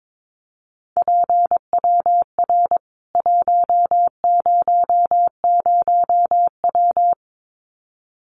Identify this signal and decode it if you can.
Morse code, heard as a keyed tone, about 22 wpm, 700 Hz. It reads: PWR 100W
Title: Common QSO Phrases 22wpm